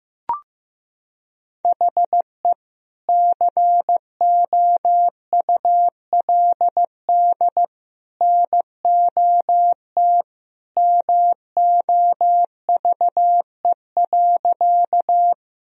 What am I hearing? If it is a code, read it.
Morse code, heard as a keyed tone, about 14 wpm, 700 Hz. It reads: HE COULD NOT MOVE.
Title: Sentences from Top 1000 Words - Review Part 1 - 15wpm